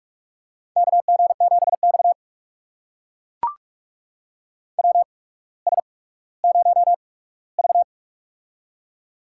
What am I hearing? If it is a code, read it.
Morse code, heard as a keyed tone, about 44 wpm, spaced as for 19 wpm, 690 Hz. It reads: KG7X WS0V
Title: USA Call Signs - 4 Characters - 8x Character Spacing - 45wpm